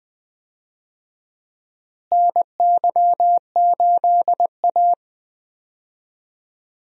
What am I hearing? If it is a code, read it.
Morse code, heard as a keyed tone, about 21 wpm, 700 Hz. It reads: NY8A
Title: USA Call Signs - 4 Characters - Repeated 3x - 20wpm